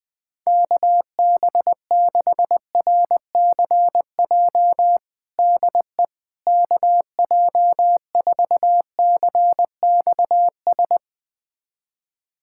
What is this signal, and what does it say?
Morse code, heard as a keyed tone, about 20 wpm, 700 Hz. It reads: KB6RCJ DE KJ4CXS
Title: USA Callsign DE Callsign - 6 Characters - 20wpm